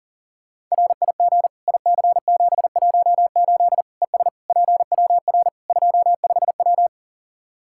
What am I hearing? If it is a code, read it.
Morse code, heard as a keyed tone, about 40 wpm, 700 Hz. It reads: RIG IC718 ES PWR 25W
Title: Practice QSO - Exchange 2 - Info - 40wpm